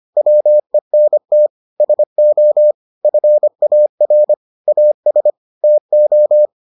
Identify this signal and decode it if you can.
Morse code, heard as a keyed tone, about 25 wpm, 600 Hz. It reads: WENT SO FAR AS TO